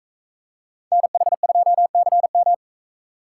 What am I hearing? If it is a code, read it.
Morse code, heard as a keyed tone, about 42 wpm, 700 Hz. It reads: NH2CM